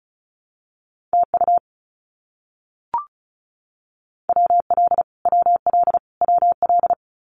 Morse code, heard as a keyed tone, about 34 wpm, 700 Hz. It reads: TU WL WL WL
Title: Abbreviations Repeated 3x - 35wpm